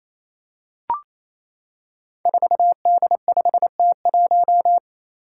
Morse code, heard as a keyed tone, about 27 wpm, 700 Hz. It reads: 4D5T1
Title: Five Challenging Letter-Number Combinations 28wpm